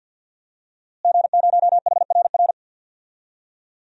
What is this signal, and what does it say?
Morse code, heard as a keyed tone, about 49 wpm, 700 Hz. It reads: G0HRR
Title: DXCC-100 Call Signs - 5 Characters - Repeated 3x - 50wpm